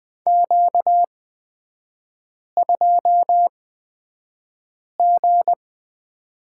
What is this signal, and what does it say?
Morse code, heard as a keyed tone, about 20 wpm, spaced as for 5 wpm, 700 Hz. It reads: Q2G
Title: 243 Lesson 39e - Letter z - Three Characters 20wpm